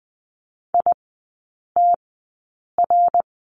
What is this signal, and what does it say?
Morse code, heard as a keyed tone, about 20 wpm, spaced as for 8 wpm, 700 Hz. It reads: ITR
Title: 100 Lesson 17e - Letter u - Three Characters 20wpm